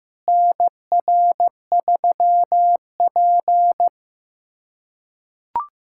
Morse code, heard as a keyed tone, about 15 wpm, 700 Hz. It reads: NR3P
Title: USA Callsign DE Callsign - 4 Characters - 15wpm